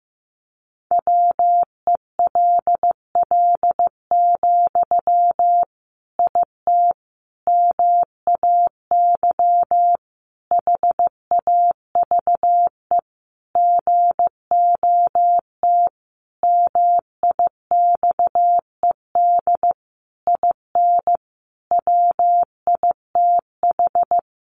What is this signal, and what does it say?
Morse code, heard as a keyed tone, about 15 wpm, 700 Hz. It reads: WELL, IT MAY HAVE GOT MIXED IN WITH
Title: Sentences from Top 1500 Words 15wpm